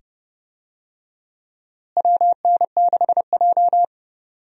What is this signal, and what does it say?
Morse code, heard as a keyed tone, about 30 wpm, 700 Hz. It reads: WN6J
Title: USA Call Signs - 4 Characters - Repeated 2x - 30wpm